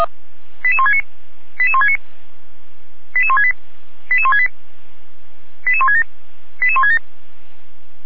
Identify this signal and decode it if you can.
Five-tone selective call (ZVEI), heard as quick runs of five tones, tones of about 70 ms. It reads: 88169 88169 88168 88168 88167 88167